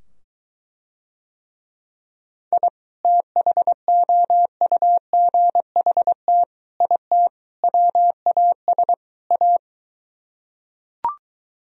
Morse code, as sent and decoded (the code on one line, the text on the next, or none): .. / - .... --- ..- --. .... - / .. - / .-- .- ... / .-
I THOUGHT IT WAS A